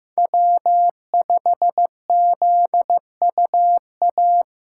Morse code, heard as a keyed tone, about 15 wpm, 700 Hz. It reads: W5ZUA
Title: USA Callsign DE Callsign 15wpm